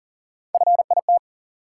Morse code, heard as a keyed tone, about 40 wpm, 700 Hz. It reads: FIT
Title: Three Letter Words - 40wpm